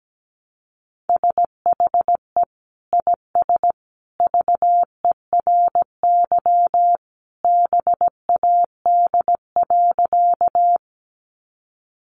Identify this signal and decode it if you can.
Morse code, heard as a keyed tone, about 17 wpm, 700 Hz. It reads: SHE IS VERY BAD.
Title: Sentences from Top 700 Words 17wpm